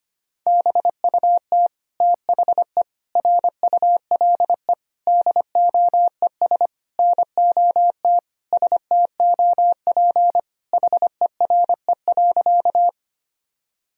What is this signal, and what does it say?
Morse code, heard as a keyed tone, about 25 wpm, 700 Hz. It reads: BUT THE RULE DOES NOT STOP HERE.
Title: Sentences from Top 1000 Words - Review Part 2 - 25wpm